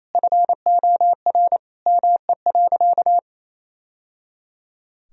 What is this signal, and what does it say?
Morse code, heard as a keyed tone, about 28 wpm, 700 Hz. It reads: FOR ME.
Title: Sentences from Top 1000 Words - Review Part 3 - 28wpm